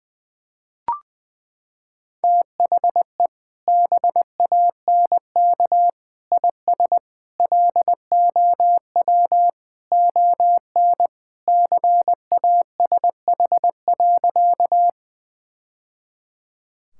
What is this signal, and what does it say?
Morse code, heard as a keyed tone, about 21 wpm, 700 Hz. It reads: THE BANK IS LOW ON CASH.
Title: Sentences from Top 1000 Words - Review Part 5 - 20wpm